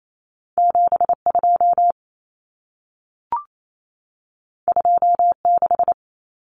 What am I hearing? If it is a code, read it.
Morse code, heard as a keyed tone, about 28 wpm, 700 Hz. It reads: 72 26
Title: Numbers - 2-Digit 28wpm